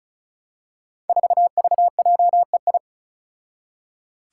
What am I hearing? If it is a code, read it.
Morse code, heard as a keyed tone, about 36 wpm, 700 Hz. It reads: 4VJEI
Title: Five Challenging Letter-Number Combinations 35wpm